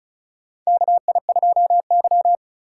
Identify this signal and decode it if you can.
Morse code, heard as a keyed tone, about 34 wpm, 700 Hz. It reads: KI2Y
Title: USA Call Signs - 4 Characters - 3x Character Spacing - 35wpm